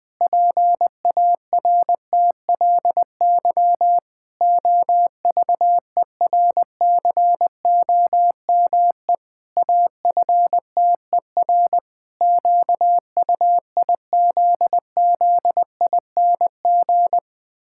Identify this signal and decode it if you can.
Morse code, heard as a keyed tone, about 19 wpm, 700 Hz. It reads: PARTLY OVERCOME AFTER QUIZZING
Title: Pangram Challenge v2 - 20wpm